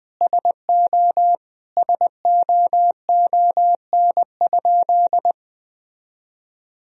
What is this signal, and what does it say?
Morse code, heard as a keyed tone, about 21 wpm, 700 Hz. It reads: SO SOON?
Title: Sentences from Top 400 Words and Abbreviations 20wpm